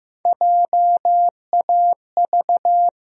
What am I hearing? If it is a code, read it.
Morse code, heard as a keyed tone, about 15 wpm, 700 Hz. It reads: JAV
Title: USA Call Sign Postfixes - 3 Letters 15wpm